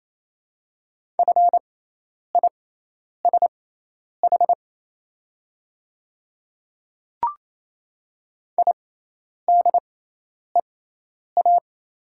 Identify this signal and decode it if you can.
Morse code, heard as a keyed tone, about 27 wpm, spaced as for 10 wpm, 700 Hz. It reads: FISH IDEA
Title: 6x Character Spacing - Top 300 Words - 28wpm